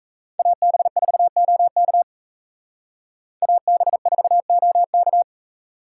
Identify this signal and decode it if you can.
Morse code, heard as a keyed tone, about 35 wpm, 700 Hz. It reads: AB4OK AB4OK